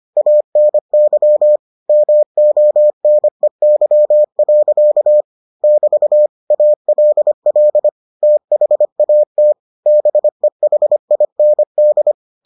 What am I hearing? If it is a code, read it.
Morse code, heard as a keyed tone, about 25 wpm, 600 Hz. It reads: ANY MONEY. <BT> ALL THAT BEHIND